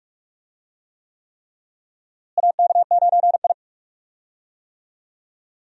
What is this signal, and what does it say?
Morse code, heard as a keyed tone, about 46 wpm, 700 Hz. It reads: AK9I
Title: USA Call Signs - 4 Characters - Repeated 3x - 45wpm